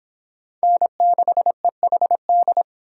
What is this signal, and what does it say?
Morse code, heard as a keyed tone, about 25 wpm, 700 Hz. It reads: N6EHD